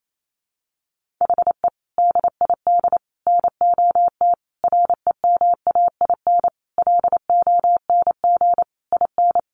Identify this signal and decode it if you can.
Morse code, heard as a keyed tone, about 28 wpm, 700 Hz. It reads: HE DID NOT REMAIN LONG IN
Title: Sentences from Top 1500 Words 28wpm